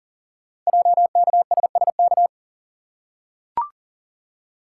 Morse code, heard as a keyed tone, about 42 wpm, 700 Hz. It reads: JKSSK
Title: Code Groups - 5 Letter:Numbers - 40wpm